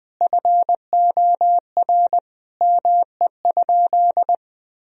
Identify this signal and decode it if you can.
Morse code, heard as a keyed tone, about 20 wpm, 700 Hz. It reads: FOR ME?
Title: Sentences from Top 1000 Words - Review Part 3 - 20wpm